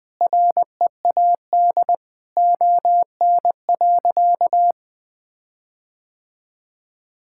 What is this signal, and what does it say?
Morse code, heard as a keyed tone, about 20 wpm, 700 Hz. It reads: READ ON.
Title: Sentences with 9-Word Phrases - 20wpm